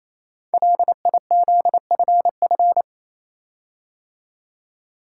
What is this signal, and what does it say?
Morse code, heard as a keyed tone, about 29 wpm, 700 Hz. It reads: LIZFF
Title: Code Groups - 5 Letters - 28wpm